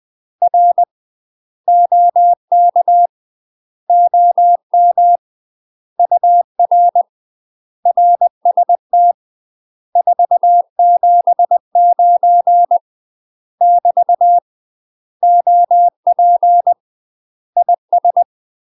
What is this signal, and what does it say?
Morse code, heard as a keyed tone, about 20 wpm, 700 Hz. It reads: R OK OM UR RST 479 <BT> OP IS
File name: UK MASTER QSOs TRAINING / 20WPM / UK MASTER QSOs 0052